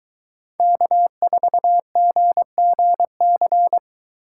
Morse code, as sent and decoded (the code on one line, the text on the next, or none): -.- ....- --. --. -.-.
K4GGC